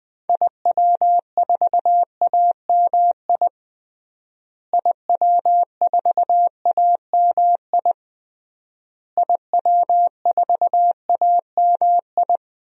Morse code, heard as a keyed tone, about 20 wpm, 700 Hz. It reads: IW4AMI IW4AMI IW4AMI
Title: DXCC-100 Call Signs - 6 Characters - Repeated 3x - 20wpm